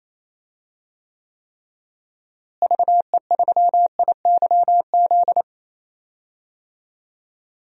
Morse code, heard as a keyed tone, about 28 wpm, 700 Hz. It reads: VE3IYZ
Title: DXCC-100 Call Signs - 6 Characters - Repeated 3x - 28wpm